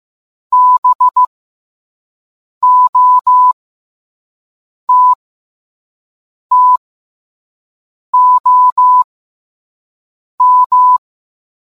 Morse code, heard as a keyed tone, about 15 wpm, spaced as for 5 wpm, 1000 Hz. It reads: BOTTOM